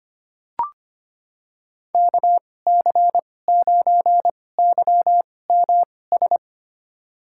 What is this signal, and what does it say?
Morse code, heard as a keyed tone, about 25 wpm, spaced as for 18 wpm, 700 Hz. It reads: KC9YMS
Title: USA Call Signs - 2x Character Spacing - 25wpm